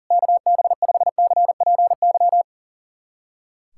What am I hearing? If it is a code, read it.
Morse code, heard as a keyed tone, about 40 wpm, 700 Hz. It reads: KB5CPY
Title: USA Callsign DE Callsign 40wpm